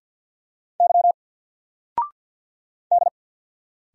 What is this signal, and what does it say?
Morse code, heard as a keyed tone, about 50 wpm, 700 Hz. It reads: Y D
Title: Single Letter-Number Alt 50wpm